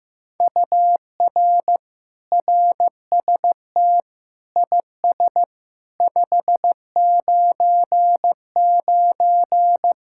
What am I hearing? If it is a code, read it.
Morse code, heard as a keyed tone, about 16 wpm, 700 Hz. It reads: UR RST IS 599
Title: Speed-Racing Common QSO Phrases Encore - 15wpm